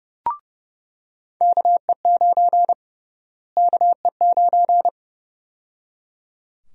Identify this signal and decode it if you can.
Morse code, heard as a keyed tone, about 31 wpm, 700 Hz. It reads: KE9 KE9
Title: USA Call Sign Prefixes - Repeated 2x - 30wpm